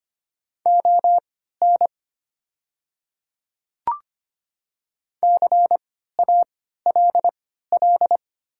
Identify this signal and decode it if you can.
Morse code, heard as a keyed tone, about 25 wpm, spaced as for 14 wpm, 700 Hz. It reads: ON CALL
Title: 3x Character Spacing - Top 100 Words - 25wpm